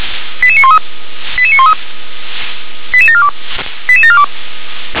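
Five-tone selective call (ZVEI), heard as quick runs of five tones, tones of about 70 ms. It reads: 88913 88913 88742 88742
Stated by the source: ZVEI five-tone selective call received via SDR.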